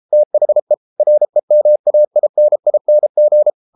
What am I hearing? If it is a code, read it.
Morse code, heard as a keyed tone, about 33 wpm, 600 Hz. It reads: THE REMAINING